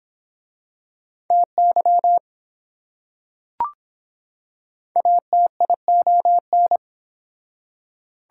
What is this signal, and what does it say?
Morse code, heard as a keyed tone, about 25 wpm, 700 Hz. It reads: TY ATION